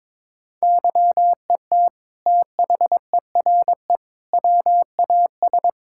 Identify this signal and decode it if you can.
Morse code, heard as a keyed tone, about 22 wpm, 700 Hz. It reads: YET THERE WAS
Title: Sentences from Top 1600 Words 22wpm